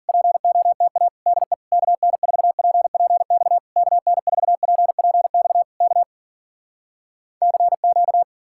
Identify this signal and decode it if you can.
Morse code, heard as a keyed tone, about 53 wpm, 700 Hz. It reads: POTA DE KN4PPX KN4PPX K CQ